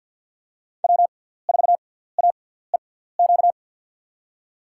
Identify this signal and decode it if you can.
Morse code, heard as a keyed tone, about 51 wpm, spaced as for 17 wpm, 690 Hz. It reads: W4AEQ
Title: USA Call Signs - 5 Characters - 6x Character Spacing - 50wpm